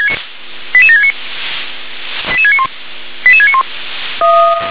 Five-tone selective call (ZVEI), heard as quick runs of five tones, tones of about 70 ms. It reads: 88769 88691 88691
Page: /357/SDR/ZVEI/88650/